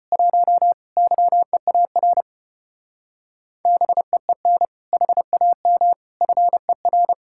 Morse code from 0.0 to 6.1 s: 1 YEAR BEEN HAM

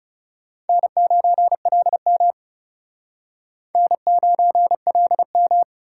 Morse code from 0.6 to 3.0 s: N9LM

